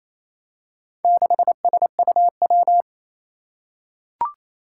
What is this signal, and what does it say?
Morse code, heard as a keyed tone, about 27 wpm, 700 Hz. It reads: 6SUW
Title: Four Challenging Letter-Number Combinations 28wpm